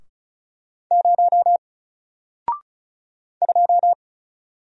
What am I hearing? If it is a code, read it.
Morse code, heard as a keyed tone, about 36 wpm, 700 Hz. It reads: 0 2
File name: Single-Letter-Number-35wpm